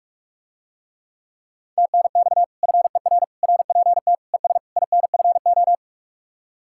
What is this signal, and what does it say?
Morse code, heard as a keyed tone, about 49 wpm, 700 Hz. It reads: TNX FER RPT ES INFO